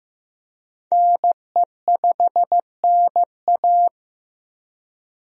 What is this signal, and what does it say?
Morse code, heard as a keyed tone, about 15 wpm, 700 Hz. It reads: NE5NA
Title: USA Call Signs - 5 Characters - Repeated 2x - 15wpm